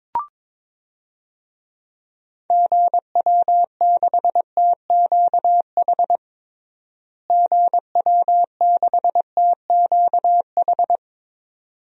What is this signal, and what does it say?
Morse code, heard as a keyed tone, about 22 wpm, 700 Hz. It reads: GW6TQH GW6TQH
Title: DXCC-100 Call Signs - 6 Characters - Repeated 2x - 22wpm